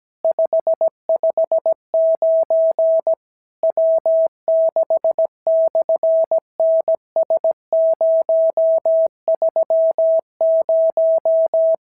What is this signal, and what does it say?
Morse code, heard as a keyed tone, about 17 wpm, 650 Hz. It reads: 559 W6/NS030
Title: SOTA QSO Practice 17wpm